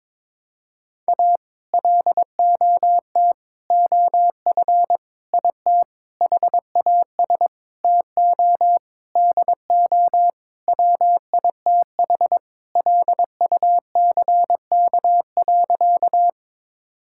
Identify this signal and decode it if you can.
Morse code, heard as a keyed tone, about 22 wpm, 700 Hz. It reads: A LOT OF IT HAS TO DO WITH LUCK.
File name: practice / 22wpm / Sentences with 8-Word Phrases - 22wpm